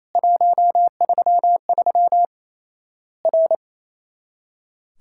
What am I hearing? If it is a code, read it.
Morse code, heard as a keyed tone, about 28 wpm, 700 Hz. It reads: 133 R
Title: Worked All Europe DX Contest - QTC Exchange - 28wpm